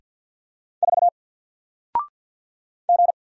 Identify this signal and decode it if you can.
Morse code, heard as a keyed tone, about 48 wpm, 700 Hz. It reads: 4 G
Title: Single Character (Letter:Number:Punctuation:Prosign) 50wpm